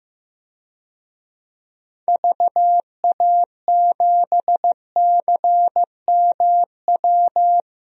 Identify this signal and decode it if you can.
Morse code, heard as a keyed tone, about 15 wpm, 700 Hz. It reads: VA7CMW
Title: DXCC-100 Call Signs - 6 Characters - Repeated 3x - 15wpm